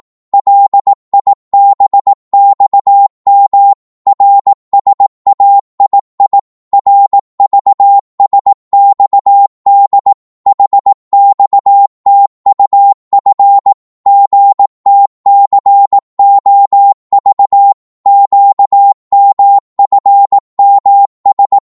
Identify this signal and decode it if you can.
Morse code, heard as a keyed tone, about 18 wpm, 800 Hz. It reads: LIBXM RSAII RVSXD HXTUF GTCOV QMFMS
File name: Morse / ltr-18wpm